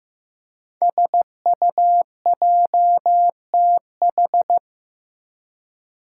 Morse code, heard as a keyed tone, about 15 wpm, 700 Hz. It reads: SUJTH
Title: Code Groups - 5 Letter:Numbers - 15wpm